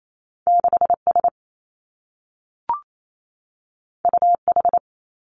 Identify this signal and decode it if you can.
Morse code, heard as a keyed tone, about 28 wpm, 700 Hz. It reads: 6 S U H
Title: Two Challenging Letter-Number Combinations 28wpm